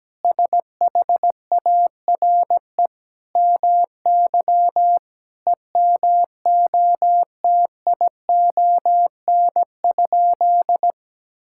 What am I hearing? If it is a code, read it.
Morse code, heard as a keyed tone, about 17 wpm, 700 Hz. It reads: SHARE MY EMOTION?
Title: Sentences from Top 1600 Words 17wpm